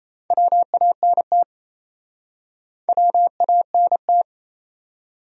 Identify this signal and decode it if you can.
Morse code, heard as a keyed tone, about 40 wpm, 700 Hz. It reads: WANT WANT